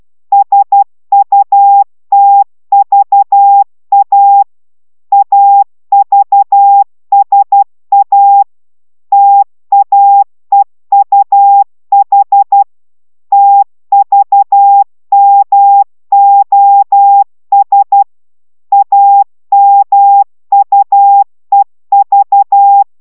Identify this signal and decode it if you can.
Morse code, heard as a keyed tone, about 12 wpm, 800 Hz. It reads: SUTVA AVSA TAEUH TVMOS AMUEV
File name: taviro tanulas / cw 69 04 10